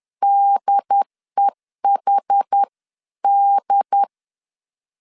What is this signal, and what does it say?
Morse code, heard as a keyed tone, about 10 wpm, 800 Hz. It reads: DEHD
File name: CW / DET / 1 3 1500C S C 10wpm 3335